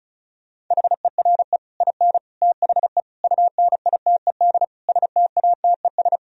Morse code, heard as a keyed tone, about 43 wpm, 700 Hz. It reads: HERE IN THE UNITED STATES
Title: Speed-Racing - Sets of 5 Words 35wpm